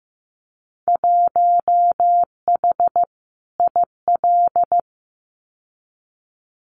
Morse code, heard as a keyed tone, about 15 wpm, 700 Hz. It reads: 1H IL
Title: Winter Field Day Exchanges 15wpm